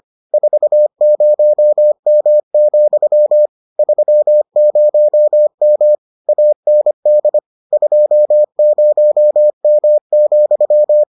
Morse code, heard as a keyed tone, about 25 wpm, 600 Hz. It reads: 40M, 30M AND 20M,